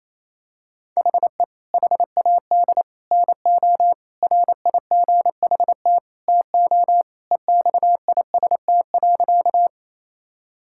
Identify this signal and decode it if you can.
Morse code, heard as a keyed tone, about 28 wpm, 700 Hz. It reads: HE HAD NO RIGHT TO EXIST.
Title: Sentences from Top 1000 Words - Review Part 5 - 28wpm